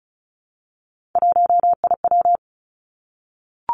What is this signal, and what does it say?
Morse code, heard as a keyed tone, about 36 wpm, 700 Hz. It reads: 1IW
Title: Three Challenging Letter-Number Combinations 35wpm